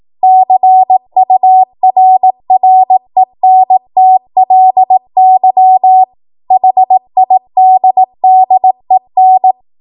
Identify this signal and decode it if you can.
Morse code, heard as a keyed tone, about 18 wpm, 750 Hz. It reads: CURRENTLY HIDDEN
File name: Morse / 18WPM